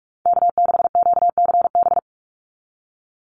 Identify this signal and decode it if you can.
Morse code, heard as a keyed tone, about 44 wpm, 700 Hz. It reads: K6QCB
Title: NA SOTA Chaser Callsigns 45wpm